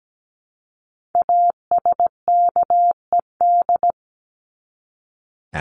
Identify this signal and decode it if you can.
Morse code, heard as a keyed tone, about 17 wpm, 700 Hz. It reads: ASKED
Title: ED Suffix - Top 100 Words with Base Words 17wpm